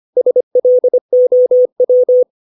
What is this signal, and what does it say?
Morse code, heard as a keyed tone, about 25 wpm, 500 Hz. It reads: SLOW
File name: 632 words / 632 Words 25WPM